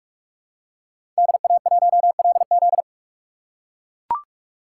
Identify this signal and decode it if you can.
Morse code, heard as a keyed tone, about 42 wpm, 700 Hz. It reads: DA1LZ
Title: USA Call Signs - 5 Characters - 6x Character Spacing - 45wpm